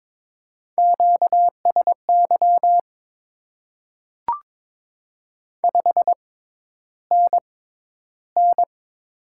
Morse code, heard as a keyed tone, about 22 wpm, spaced as for 11 wpm, 700 Hz. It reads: QSY 5NN
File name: Top 90 QSO Elements - 6x Character Spacing - 22wpm